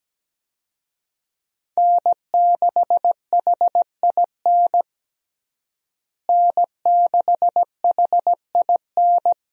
Morse code, heard as a keyed tone, about 17 wpm, 700 Hz. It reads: N6HIN N6HIN
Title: USA Call Signs - 5 Characters - Repeated 2x - 17wpm